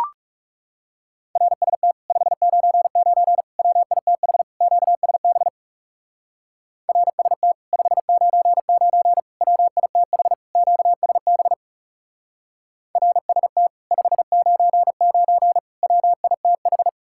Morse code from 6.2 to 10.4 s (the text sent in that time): RST 599 WITH